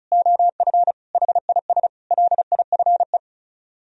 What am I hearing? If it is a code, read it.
Morse code, heard as a keyed tone, about 32 wpm, 700 Hz. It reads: OF HIS LIFE
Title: Speed-Racing - Sets of 5 Words 30wpm